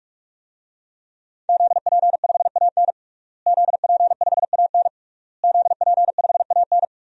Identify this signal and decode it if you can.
Morse code, heard as a keyed tone, about 41 wpm, 700 Hz. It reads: ZP5AN ZP5AN ZP5AN
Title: DXCC-100 Call Signs - 5 Characters - Repeated 3x - 45wpm